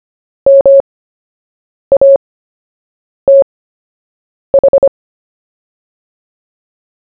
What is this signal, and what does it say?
Morse code, heard as a keyed tone, about 25 wpm, spaced as for 7 wpm, 550 Hz. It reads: MATH
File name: cw / fists / Short-Story-1-7wpm